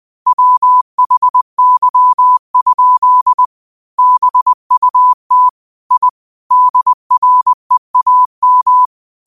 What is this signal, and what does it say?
Morse code, heard as a keyed tone, about 20 wpm, 1000 Hz. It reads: WHY? BUT I DREAM
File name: qotdfiles / 20wpm